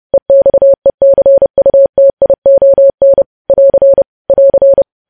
Morse code, heard as a keyed tone, about 30 wpm, 550 Hz. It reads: EXECUTION <AR> <AR>
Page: news headlines in morse code at 30 wpm